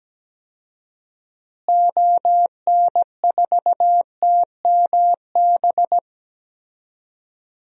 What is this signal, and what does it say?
Morse code, heard as a keyed tone, about 17 wpm, 700 Hz. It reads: ON4TMB
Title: DXCC-100 Call Signs - 6 Characters - Repeated 2x - 17wpm